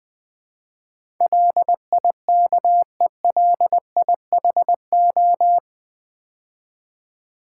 Morse code, heard as a keyed tone, about 20 wpm, 700 Hz. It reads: LIKELIHO
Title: Build A Word - 4 Letters At A Time - 10-Letter Words - 20wpm